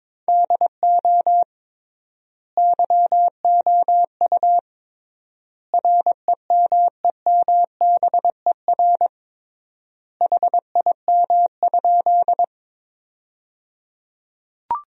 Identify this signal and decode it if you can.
Morse code, heard as a keyed tone, about 22 wpm, 700 Hz. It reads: DO YOU REMEMBER HIM?
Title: 3x Word-Spacing - Sentences from Top 400 Words - 22wpm